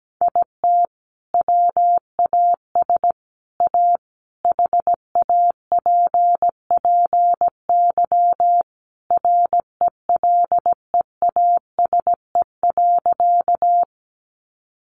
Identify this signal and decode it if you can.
Morse code, heard as a keyed tone, about 17 wpm, 700 Hz. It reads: IT WAS A HAPPY RELEASE.
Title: Sentences from Top 1100 Words 17wpm